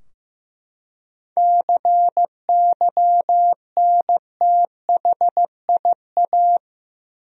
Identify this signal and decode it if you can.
Morse code, heard as a keyed tone, about 15 wpm, 700 Hz. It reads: CYNTHIA
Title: USA Common Names - 15wpm